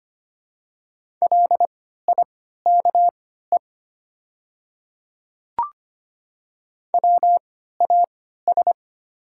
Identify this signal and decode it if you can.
Morse code, heard as a keyed tone, about 25 wpm, spaced as for 14 wpm, 690 Hz. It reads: LIKE WAS
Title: 3x Character Spacing - Top 100 Words - 25wpm